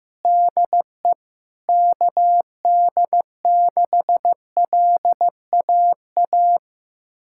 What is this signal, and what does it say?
Morse code, heard as a keyed tone, about 16 wpm, 700 Hz. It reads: DE KD6LAA
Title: USA Callsign DE Callsign - 6 Characters - 15wpm